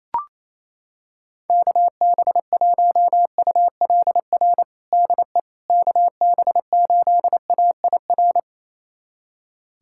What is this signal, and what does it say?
Morse code, heard as a keyed tone, about 28 wpm, 700 Hz. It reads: KB1ULR DE KB8AIR
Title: USA Callsign DE Callsign 28wpm